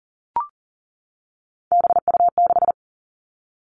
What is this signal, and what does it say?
Morse code, heard as a keyed tone, about 42 wpm, 700 Hz. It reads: BU6
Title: Three Challenging Letter-Number Combinations 40wpm